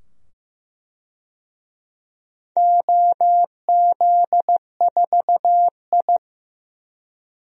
Morse code, heard as a keyed tone, about 15 wpm, 700 Hz. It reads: OZ4I
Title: DXCC-100 Call Signs - 4 Characters - 15wpm